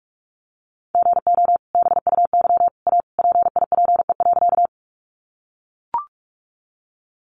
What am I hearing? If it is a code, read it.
Morse code, heard as a keyed tone, about 44 wpm, 700 Hz. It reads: GO BUY A PIPE.
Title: Sentences from Top 1500 Words 45wpm